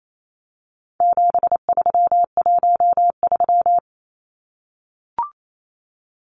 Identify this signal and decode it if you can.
Morse code, heard as a keyed tone, about 28 wpm, 700 Hz. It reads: 7313
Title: Numbers - 4-Digit 28wpm